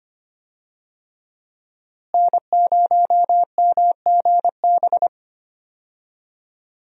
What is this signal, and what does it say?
Morse code, heard as a keyed tone, about 25 wpm, 700 Hz. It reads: N0MGB
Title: USA Call Signs - 5 Characters - Repeated 3x - 25wpm